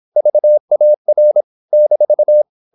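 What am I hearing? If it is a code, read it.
Morse code, heard as a keyed tone, about 26 wpm, 600 Hz. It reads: VAR -
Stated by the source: High-speed (QRQ) Morse at 26 wpm.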